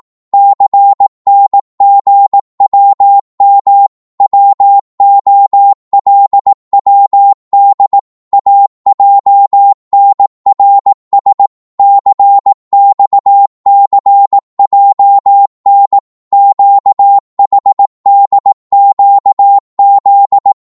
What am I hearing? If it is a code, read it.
Morse code, heard as a keyed tone, about 18 wpm, 800 Hz. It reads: CNGWM WOLWD AJNRS CXCJN QHDQZ